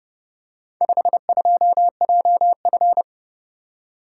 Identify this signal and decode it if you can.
Morse code, heard as a keyed tone, about 31 wpm, 700 Hz. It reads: 52JF